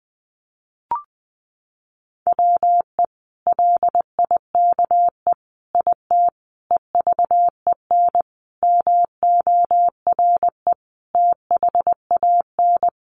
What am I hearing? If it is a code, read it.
Morse code, heard as a keyed tone, about 20 wpm, 700 Hz. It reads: WE LIKE IT EVEN MORE THAN
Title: Sentences from Top 1100 Words 20wpm